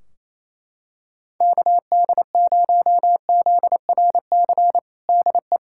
Morse code, heard as a keyed tone, about 28 wpm, 700 Hz. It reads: KD0ZRC DE
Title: Practice QSO - Exchange 4 - Ending - 28wpm